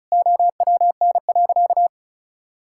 Morse code, heard as a keyed tone, about 36 wpm, 700 Hz. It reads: OWN.
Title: Sentences from Top 1000 Words - Review Part 6 - 35wpm